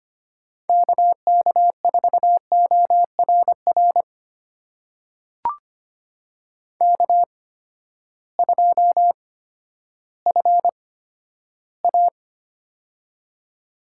Morse code, heard as a keyed tone, about 25 wpm, spaced as for 13 wpm, 700 Hz. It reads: KK4ORR K2FA
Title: USA Call Signs - 8x Character Spacing - 25wpm